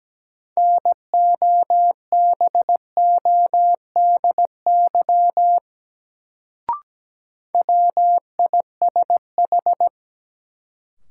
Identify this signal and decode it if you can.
Morse code, heard as a keyed tone, about 17 wpm, 700 Hz. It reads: NOBODY WISH
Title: Sentences from Top 1600 Words - New Word Drill - 17wpm